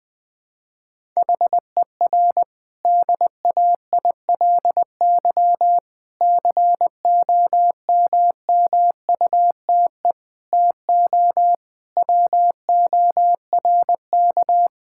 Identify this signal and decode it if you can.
Morse code, heard as a keyed tone, about 20 wpm, 700 Hz. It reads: HER DAILY COMMUTE TO WORK
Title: Difficult Sentences with Hints - 20wpm